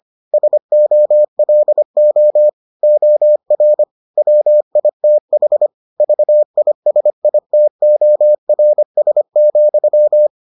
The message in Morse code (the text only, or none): SOLO OR WITH VISITORS,